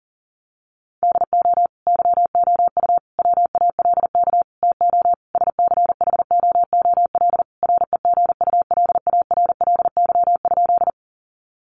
Morse code, heard as a keyed tone, about 40 wpm, 700 Hz. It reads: DO YOU WALK TO SCHOOL REGULARLY?
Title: Sentences from Top 1400 Words 40wpm